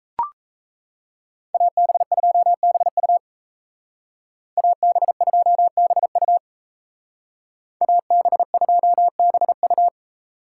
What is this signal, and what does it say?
Morse code, heard as a keyed tone, about 33 wpm, 700 Hz. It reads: AB2BU AB2BU AB2BU
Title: Speed Racing - USA Call Signs 28wpm